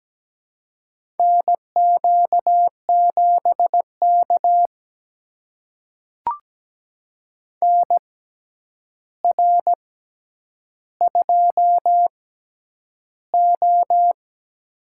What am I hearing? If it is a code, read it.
Morse code, heard as a keyed tone, about 17 wpm, spaced as for 9 wpm, 700 Hz. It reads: NQ7K NR2O
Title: USA Call Signs - 4 Characters - 6x Character Spacing - 17wpm